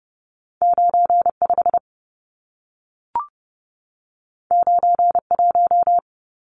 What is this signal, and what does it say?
Morse code, heard as a keyed tone, about 30 wpm, 700 Hz. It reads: 95 91
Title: Numbers - 2-Digit 30wpm